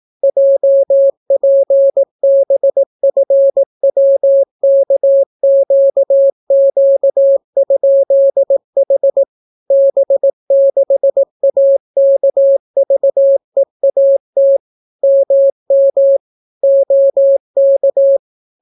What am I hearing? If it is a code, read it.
Morse code, heard as a keyed tone, about 18 wpm, 550 Hz. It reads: JPBFWKQQ?H B6AKVEAT MM OK